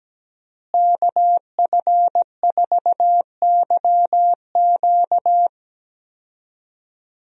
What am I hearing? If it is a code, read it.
Morse code, heard as a keyed tone, about 17 wpm, 700 Hz. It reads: KF4YQ
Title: USA Call Signs - 5 Characters - Repeated 2x - 17wpm